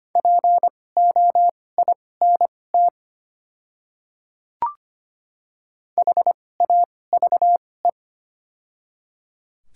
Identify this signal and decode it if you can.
Morse code, heard as a keyed tone, about 25 wpm, spaced as for 18 wpm, 700 Hz. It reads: POINT HAVE
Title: 2x Character Spacing - Top 300 Words - 25wpm